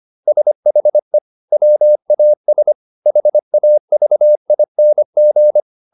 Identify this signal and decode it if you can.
Morse code, heard as a keyed tone, about 25 wpm, 600 Hz. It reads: SHE WAS HAVING
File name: SS123 25